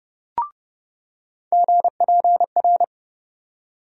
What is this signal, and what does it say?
Morse code, heard as a keyed tone, about 30 wpm, 700 Hz. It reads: GPR
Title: USA Call Sign Postfixes - 3 Letters 30wpm